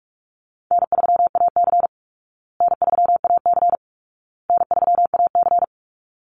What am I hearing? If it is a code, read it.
Morse code, heard as a keyed tone, about 44 wpm, 700 Hz. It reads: N3AC N3AC N3AC
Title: Easy Call Signs Repeated 3x - 45wpm